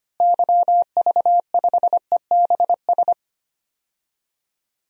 Morse code, heard as a keyed tone, about 25 wpm, 700 Hz. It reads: YV5EBS
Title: DXCC-100 Call Signs - 6 Characters - Repeated 2x - 25wpm